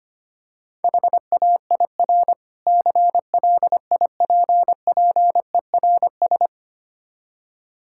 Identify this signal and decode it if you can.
Morse code, heard as a keyed tone, about 25 wpm, 700 Hz. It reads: HAIR CLIPPERS
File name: Difficult Words - Speed-Racing with Hints - 25wpm